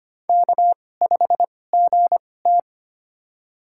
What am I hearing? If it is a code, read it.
Morse code, heard as a keyed tone, about 25 wpm, spaced as for 18 wpm, 700 Hz. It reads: K5GT
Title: USA Call Signs - 4 Characters - 2x Character Spacing - 25wpm